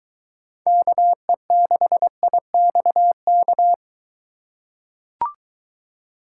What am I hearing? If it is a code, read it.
Morse code, heard as a keyed tone, about 23 wpm, 700 Hz. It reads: KE6IXK